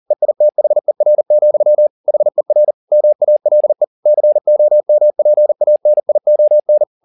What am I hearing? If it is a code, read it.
Morse code, heard as a keyed tone, about 40 wpm, 600 Hz. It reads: EITHER, HER MALE COMPANION
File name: the old man and the sea 2 40wpm